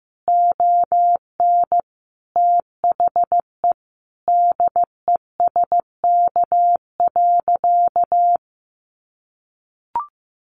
Morse code, heard as a keyed tone, about 14 wpm, 700 Hz. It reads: ON THE DESK.
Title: Additional Easy Phrases 15wpm